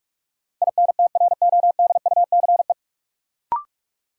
Morse code, heard as a keyed tone, about 41 wpm, 700 Hz. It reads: INTRODUCE
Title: Sentences from Top 1600 Words - New Word Drill Alt - 45wpm